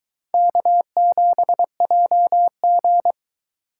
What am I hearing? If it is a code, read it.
Morse code, heard as a keyed tone, about 23 wpm, 700 Hz. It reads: K7JG